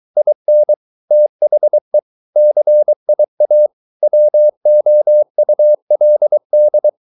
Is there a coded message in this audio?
Yes: Morse code from 0.2 to 6.9 s, heard as a keyed tone, about 23 wpm, 600 Hz. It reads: IN THE CIA WOULD